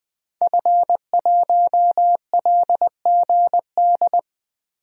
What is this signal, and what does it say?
Morse code, heard as a keyed tone, about 21 wpm, 700 Hz. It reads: F1LGD
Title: DXCC-100 Call Signs - 5 Characters - Repeated 3x - 20wpm